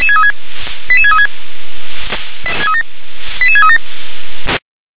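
Five-tone selective call (ZVEI), heard as quick runs of five tones, tones of about 70 ms. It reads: 88636 88637
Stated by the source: ZVEI five-tone selective call 88730 received by SDR.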